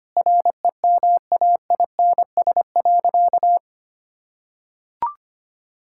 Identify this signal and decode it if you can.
Morse code, heard as a keyed tone, about 25 wpm, 700 Hz. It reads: REMAINS.
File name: practice / 25wpm / Sentences with 2-Word Phrases - 25wpm